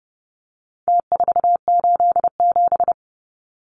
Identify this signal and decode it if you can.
Morse code, heard as a keyed tone, about 30 wpm, 700 Hz. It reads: T487
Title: Numbers - Contest Style - 4-Digit 30wpm